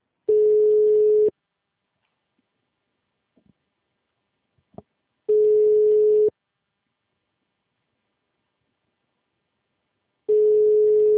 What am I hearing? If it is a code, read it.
Ringback tone (Europe, 425 Hz).